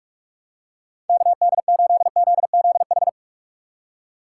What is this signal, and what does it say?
Morse code, heard as a keyed tone, about 42 wpm, 700 Hz. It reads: KD8ZZH
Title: USA Call Signs - 6 Characters - 4x Character Spacing - 45wpm